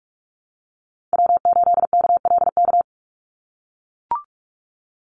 Morse code, heard as a keyed tone, about 44 wpm, 700 Hz. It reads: W8KLK
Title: USA Call Signs 45wpm